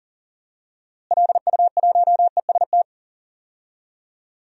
Morse code, heard as a keyed tone, about 40 wpm, 700 Hz. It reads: LU1EST
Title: DXCC-100 Call Signs - 6 Characters - Repeated 3x - 40wpm